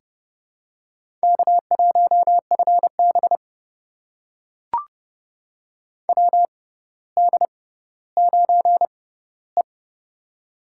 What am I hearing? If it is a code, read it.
Morse code, heard as a keyed tone, about 31 wpm, spaced as for 16 wpm, 700 Hz. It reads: K1FB WD9E
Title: USA Call Signs - 4 Characters - 6x Character Spacing - 30wpm